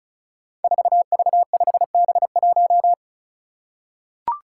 Morse code, heard as a keyed tone, about 36 wpm, 700 Hz. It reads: 4V5B1
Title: Five Challenging Letter-Number Combinations 35wpm